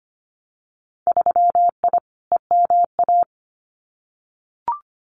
Morse code, heard as a keyed tone, about 25 wpm, 700 Hz. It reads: 3I EMA
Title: Winter Field Day Exchanges Encore 25wpm